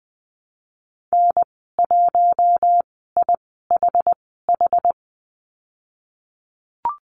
Morse code, heard as a keyed tone, about 20 wpm, spaced as for 15 wpm, 700 Hz. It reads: N1IHH
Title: 088 Lesson 15g - Number 5 - Call Signs 20wpm